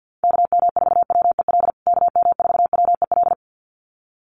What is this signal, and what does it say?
Morse code, heard as a keyed tone, about 50 wpm, 700 Hz. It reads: KM4WEL KM4WEL
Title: USA Call Signs Repeated 2x 50wpm